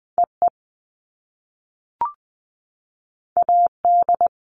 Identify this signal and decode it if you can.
Morse code, heard as a keyed tone, about 20 wpm, 700 Hz. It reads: E E A D
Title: All Cut Numbers - 2-Digit 20wpm